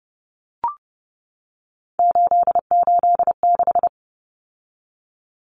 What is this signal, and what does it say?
Morse code, heard as a keyed tone, about 30 wpm, 700 Hz. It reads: 886
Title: Numbers - 3-Digit 30wpm